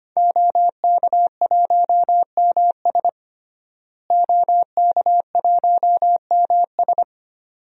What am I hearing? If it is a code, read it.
Morse code, heard as a keyed tone, about 25 wpm, 700 Hz. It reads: OK1MS OK1MS
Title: DXCC-100 Call Signs - 5 Characters - Repeated 2x - 25wpm